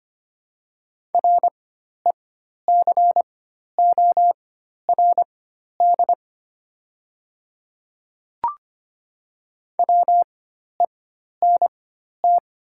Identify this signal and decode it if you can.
Morse code, heard as a keyed tone, about 25 wpm, spaced as for 12 wpm, 700 Hz. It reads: RECORD WENT
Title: 4x Character Spacing - Top 500 Words - 25wpm